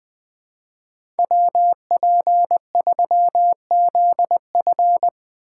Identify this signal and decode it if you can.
Morse code, heard as a keyed tone, about 21 wpm, 700 Hz. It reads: WP3ZF
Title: USA Call Signs - 5 Characters - 4x Character Spacing - 20wpm